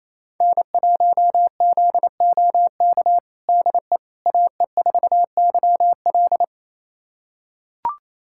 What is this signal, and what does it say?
Morse code, heard as a keyed tone, about 28 wpm, 700 Hz. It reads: N1ZOK DE AE4YL
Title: USA Callsign DE Callsign - 5 Characters - 28wpm